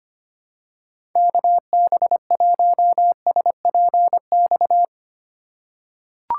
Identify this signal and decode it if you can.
Morse code, heard as a keyed tone, about 25 wpm, 700 Hz. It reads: KB1SPX
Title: USA Call Signs - 6 Characters - 2x Character Spacing - 25wpm